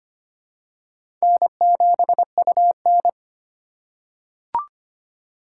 Morse code, heard as a keyed tone, about 25 wpm, 700 Hz. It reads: N7UN
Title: USA Call Signs - 4 Characters - 4x Character Spacing - 25wpm